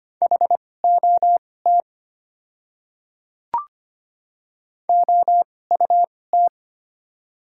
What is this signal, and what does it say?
Morse code, heard as a keyed tone, about 25 wpm, spaced as for 18 wpm, 700 Hz. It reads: HOT OUT
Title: 2x Character Spacing - Top 100 Words - 25wpm